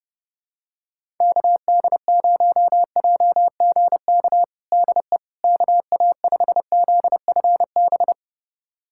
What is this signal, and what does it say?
Morse code, heard as a keyed tone, about 30 wpm, 700 Hz. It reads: KD0JGK DE KA5ZFB
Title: USA Callsign DE Callsign - 6 Characters - 30wpm